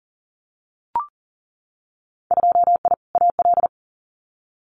Morse code, heard as a keyed tone, about 40 wpm, 700 Hz. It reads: 2I AL
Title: Winter Field Day Exchanges 40wpm